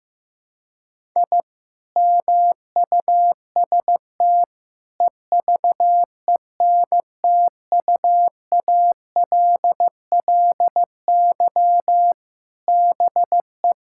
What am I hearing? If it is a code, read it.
Morse code, heard as a keyed tone, about 16 wpm, 700 Hz. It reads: I MUST EVENTUALLY BE
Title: Sentences from Top 1000 Words - Review Part 6 - 15wpm